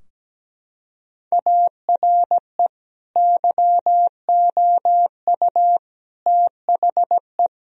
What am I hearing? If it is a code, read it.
Morse code, heard as a keyed tone, about 17 wpm, 700 Hz. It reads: ARE YOU THE
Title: Sentences from Top 1000 Words - Review Part 4 - 17wpm